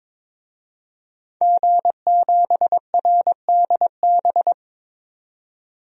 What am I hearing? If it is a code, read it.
Morse code, heard as a keyed tone, about 22 wpm, 700 Hz. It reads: G7RDB
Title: DXCC-100 Call Signs - 5 Characters - 22wpm